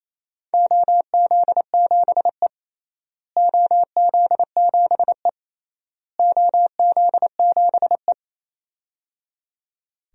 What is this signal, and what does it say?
Morse code, heard as a keyed tone, about 28 wpm, 700 Hz. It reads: OZ7E OZ7E OZ7E
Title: DXCC-100 Call Signs - 4 Characters - Repeated 3x - 28wpm